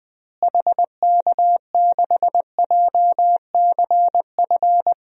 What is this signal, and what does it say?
Morse code, heard as a keyed tone, about 21 wpm, 700 Hz. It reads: HK6JCF
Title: DXCC-100 Call Signs - 6 Characters - Repeated 2x - 20wpm